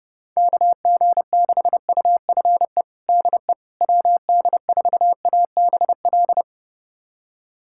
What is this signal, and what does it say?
Morse code, heard as a keyed tone, about 32 wpm, 700 Hz. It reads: KG6UFE DE WD4ABL
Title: USA Callsign DE Callsign 30wpm